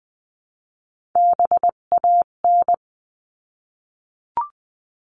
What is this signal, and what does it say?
Morse code, heard as a keyed tone, about 20 wpm, 700 Hz. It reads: BAN
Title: 258 Lesson 42c - Words 16-20wpm